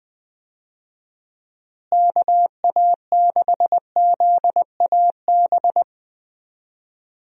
Morse code, heard as a keyed tone, about 21 wpm, 700 Hz. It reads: KA6ZAB
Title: USA Call Signs - 6 Characters - Repeated 2x - 20wpm